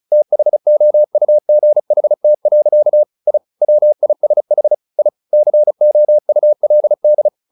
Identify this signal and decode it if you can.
Morse code, heard as a keyed tone, about 34 wpm, 600 Hz. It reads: THOUGHT. I WISH I COULD
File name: the old man and the sea / the old man and the sea 1 35wpm